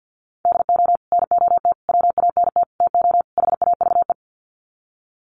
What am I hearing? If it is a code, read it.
Morse code, heard as a keyed tone, about 50 wpm, 700 Hz. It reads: DO NOT WANT TO HAVE
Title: Sets of 5 Words 50wpm